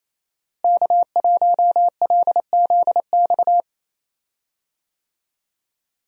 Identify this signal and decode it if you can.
Morse code, heard as a keyed tone, about 27 wpm, 700 Hz. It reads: K1LZX
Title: USA Call Signs - 5 Characters - Repeated 2x - 28wpm